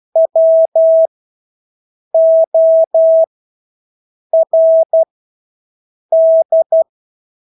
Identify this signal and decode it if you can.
Morse code, heard as a keyed tone, about 12 wpm, spaced as for 6 wpm, 650 Hz. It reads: WORD
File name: Engelske ord 12-6 wpm